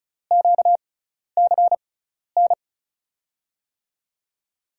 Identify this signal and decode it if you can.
Morse code, heard as a keyed tone, about 36 wpm, spaced as for 12 wpm, 700 Hz. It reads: QCN
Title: USA Call Signs - 6 Characters - 6x Character Spacing - 35wpm